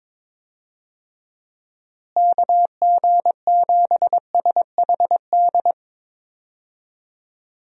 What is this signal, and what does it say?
Morse code, heard as a keyed tone, about 22 wpm, 700 Hz. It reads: KG7SHD
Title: USA Call Signs - 6 Characters - Repeated 2x - 22wpm